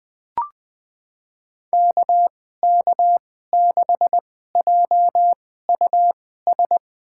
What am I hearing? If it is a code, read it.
Morse code, heard as a keyed tone, about 20 wpm, spaced as for 14 wpm, 700 Hz. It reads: KK6JUS
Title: USA Call Signs - 6 Characters - 2x Character Spacing - 20wpm